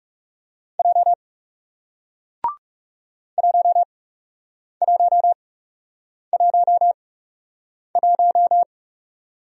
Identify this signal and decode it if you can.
Morse code, heard as a keyed tone, about 37 wpm, spaced as for 8 wpm, 700 Hz. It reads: J 1111
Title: Speed-Racing Single Letter-Number 30wpm